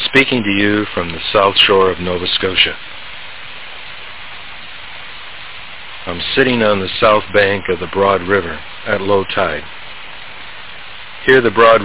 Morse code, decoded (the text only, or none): none